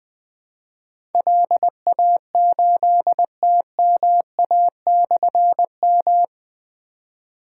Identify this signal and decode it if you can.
Morse code, heard as a keyed tone, about 20 wpm, 700 Hz. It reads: LA8TMA/M
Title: Difficult Call Signs 20wpm